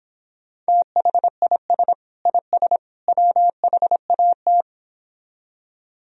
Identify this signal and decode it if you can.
Morse code, heard as a keyed tone, about 29 wpm, 700 Hz. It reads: THIS IS WHAT